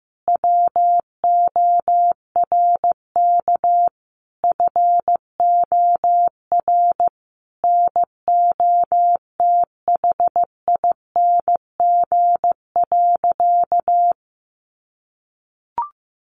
Morse code, heard as a keyed tone, about 15 wpm, 700 Hz. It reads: WORK FOR NOTHING.
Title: Sentences from Top 1400 Words 15wpm